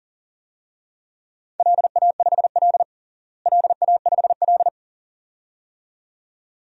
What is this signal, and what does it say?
Morse code, heard as a keyed tone, about 37 wpm, 700 Hz. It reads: LA5L LA5L
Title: DXCC-100 Call Signs - 4 Characters - Repeated 2x - 40wpm